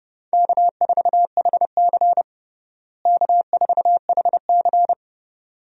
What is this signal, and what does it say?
Morse code, heard as a keyed tone, about 31 wpm, 700 Hz. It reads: K4HC K4HC
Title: USA Call Signs - 4 Characters - Repeated 2x - 30wpm